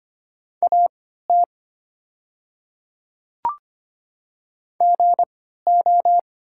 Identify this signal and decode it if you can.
Morse code, heard as a keyed tone, about 25 wpm, spaced as for 14 wpm, 700 Hz. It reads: AT GO
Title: 3x Character Spacing - Top 100 Words - 25wpm